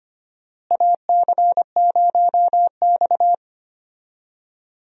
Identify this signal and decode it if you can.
Morse code, heard as a keyed tone, about 25 wpm, 700 Hz. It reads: AC0X
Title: USA Call Signs - 4 Characters - Repeated 3x - 25wpm